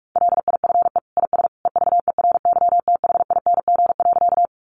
Morse code, heard as a keyed tone, about 44 wpm, 700 Hz. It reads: LIFE IS EVERYTHING.
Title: Sentences from Top 700 Words 45wpm